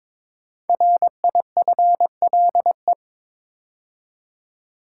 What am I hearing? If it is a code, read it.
Morse code, heard as a keyed tone, about 22 wpm, 700 Hz. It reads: RIFLE
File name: 5 Letter Words - First 3 Letters Given - 22wpm